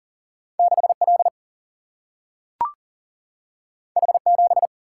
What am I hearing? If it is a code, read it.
Morse code, heard as a keyed tone, about 40 wpm, 700 Hz. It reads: 6 L H 7
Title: Single-Letter Single-Number 40wpm